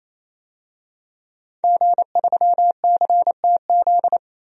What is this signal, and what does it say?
Morse code, heard as a keyed tone, about 28 wpm, 700 Hz. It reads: G3CTZ
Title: DXCC-100 Call Signs - 5 Characters - 28wpm